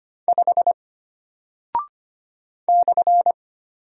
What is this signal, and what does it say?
Morse code, heard as a keyed tone, about 25 wpm, 700 Hz. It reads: 5 /
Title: Single Character (Letter:Number:Punctuation:Prosign) Alt 25wpm